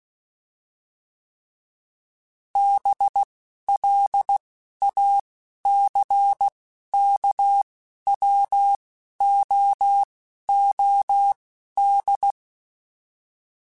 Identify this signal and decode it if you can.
Morse code, heard as a keyed tone, about 16 wpm, spaced as for 11 wpm, 800 Hz. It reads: BLACKWOOD
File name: fromSMs / FromSMs mp3s / Cities@10